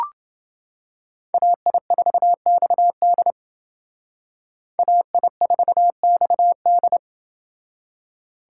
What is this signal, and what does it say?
Morse code, heard as a keyed tone, about 28 wpm, 700 Hz. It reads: AI4XD AI4XD